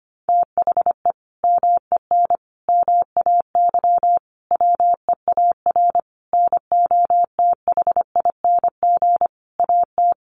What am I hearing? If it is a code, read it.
Morse code, heard as a keyed tone, about 25 wpm, 700 Hz. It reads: THE MEN MAY WEAR NOTHING AT
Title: Sentences from Top 1500 Words 25wpm